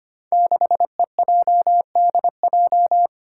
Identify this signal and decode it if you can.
Morse code, heard as a keyed tone, about 25 wpm, 700 Hz. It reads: 6EJDJ
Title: Five Challenging Letter-Number Combinations 25wpm